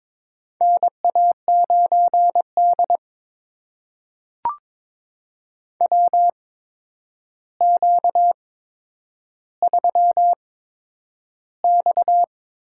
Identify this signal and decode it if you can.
Morse code, heard as a keyed tone, about 22 wpm, spaced as for 9 wpm, 700 Hz. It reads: NA9D WQ3X
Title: USA Call Signs - 8x Character Spacing - 22wpm